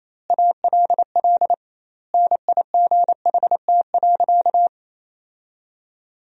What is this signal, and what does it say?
Morse code, heard as a keyed tone, about 28 wpm, 700 Hz. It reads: ALL NIGHT.
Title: 2x Word-Spacing - Sentences from Top 500 Words - 28wpm